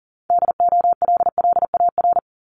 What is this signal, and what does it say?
Morse code, heard as a keyed tone, about 38 wpm, 700 Hz. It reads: DOLLAR
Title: Multi-letter Phonemes Repeated 3x 40wpm